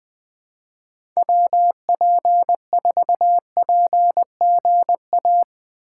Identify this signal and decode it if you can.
Morse code, heard as a keyed tone, about 20 wpm, 700 Hz. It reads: WP4PGA
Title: USA Call Signs - 6 Characters - 6x Character Spacing - 20wpm